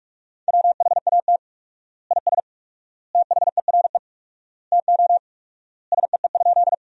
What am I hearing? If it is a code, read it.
Morse code, heard as a keyed tone, about 42 wpm, 700 Hz. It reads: WHAT IS THERE TO SEE?
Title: 4x Word-Spacing - Sentences from Top 100 Words - 45wpm